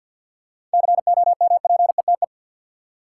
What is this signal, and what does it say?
Morse code, heard as a keyed tone, about 53 wpm, 700 Hz. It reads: COMPETE
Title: Seven Letter Words - 50wpm